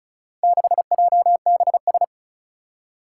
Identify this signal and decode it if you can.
Morse code, heard as a keyed tone, about 36 wpm, 700 Hz. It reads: 6JBS
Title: Four Challenging Letter-Number Combinations 35wpm